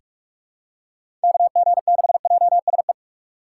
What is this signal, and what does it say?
Morse code, heard as a keyed tone, about 44 wpm, 700 Hz. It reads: KG6JSE
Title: USA Call Signs - 6 Characters - 4x Character Spacing - 45wpm